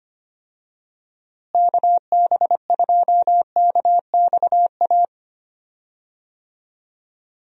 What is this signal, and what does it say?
Morse code, heard as a keyed tone, about 25 wpm, 700 Hz. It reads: KB2KXA
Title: USA Call Signs - 6 Characters - Repeated 3x - 25wpm